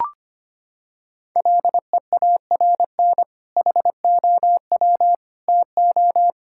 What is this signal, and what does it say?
Morse code, heard as a keyed tone, about 25 wpm, 700 Hz. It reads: LEARN HOW TO
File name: practice / 25wpm / Sentences with 2-Word Phrases - 25wpm